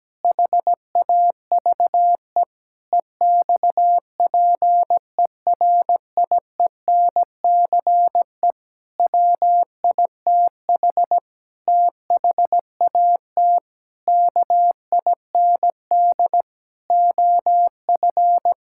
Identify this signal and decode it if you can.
Morse code, heard as a keyed tone, about 17 wpm, 700 Hz. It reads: HAVE EXPERIENCE WITH THAT KIND OF
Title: Sentences from Top 1000 Words - Review Part 5 - 17wpm